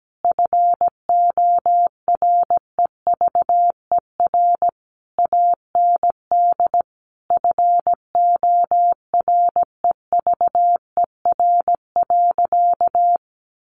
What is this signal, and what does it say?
Morse code, heard as a keyed tone, about 17 wpm, 700 Hz. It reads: FOREVER AND FOREVER.
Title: A Sound Of Thunder 17wpm